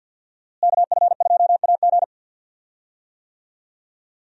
Morse code, heard as a keyed tone, about 50 wpm, 700 Hz. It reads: KF2AG
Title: USA Call Signs - 5 Characters - Repeated 3x - 50wpm